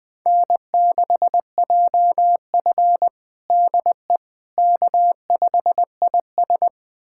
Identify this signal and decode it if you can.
Morse code, heard as a keyed tone, about 21 wpm, 700 Hz. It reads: N6JF DE K5IS
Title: USA Callsign DE Callsign - 4 Characters - 20wpm